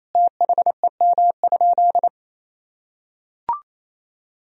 Morse code, heard as a keyed tone, about 27 wpm, 700 Hz. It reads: THEM?
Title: Sentences from Top 1000 Words - Review Part 3 - 28wpm